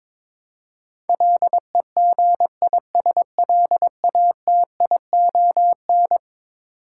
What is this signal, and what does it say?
Morse code, heard as a keyed tone, about 22 wpm, 700 Hz. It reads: LEGISLATION
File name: practice / 22wpm / Eleven Letter Words - 22wpm